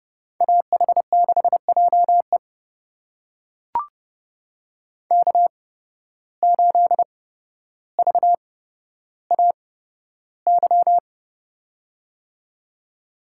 Morse code, heard as a keyed tone, about 31 wpm, spaced as for 13 wpm, 700 Hz. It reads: AH6JE K8VAY
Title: USA Call Signs - 5 Characters - 8x Character Spacing - 30wpm